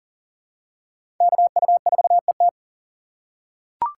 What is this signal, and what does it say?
Morse code, heard as a keyed tone, about 40 wpm, 700 Hz. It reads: KU4ET
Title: USA Call Signs - 5 Characters - 2x Character Spacing - 40wpm